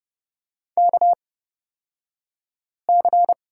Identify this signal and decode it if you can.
Morse code, heard as a keyed tone, about 31 wpm, spaced as for 5 wpm, 700 Hz. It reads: KC
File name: Lesson 36d - BT - Two Characters 30wpm